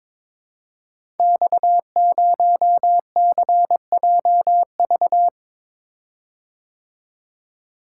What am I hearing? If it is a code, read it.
Morse code, heard as a keyed tone, about 22 wpm, 700 Hz. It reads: X0CJV
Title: Code Groups - 5 Letter:Numbers - 22wpm